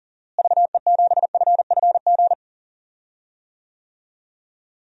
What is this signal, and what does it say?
Morse code, heard as a keyed tone, about 37 wpm, 700 Hz. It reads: VE7FFG
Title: DXCC-100 Call Signs - 6 Characters - Repeated 2x - 40wpm